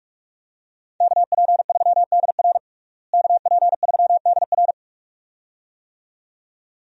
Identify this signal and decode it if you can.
Morse code, heard as a keyed tone, about 46 wpm, 700 Hz. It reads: KP3DR KP3DR
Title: USA Call Signs - 5 Characters - Repeated 2x - 45wpm